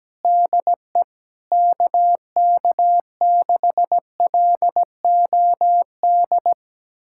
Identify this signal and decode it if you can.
Morse code, heard as a keyed tone, about 17 wpm, 700 Hz. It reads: DE KK6LOD
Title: USA Callsign DE Callsign - 6 Characters - 17wpm